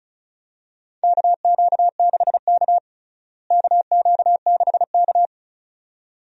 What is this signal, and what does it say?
Morse code, heard as a keyed tone, about 36 wpm, 700 Hz. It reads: KQ6K KQ6K
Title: USA Call Signs - 4 Characters - Repeated 2x - 35wpm